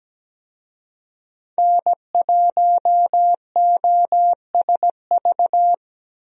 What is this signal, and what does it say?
Morse code, heard as a keyed tone, about 17 wpm, 700 Hz. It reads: N1OSV
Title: USA Call Signs - 5 Characters - 17wpm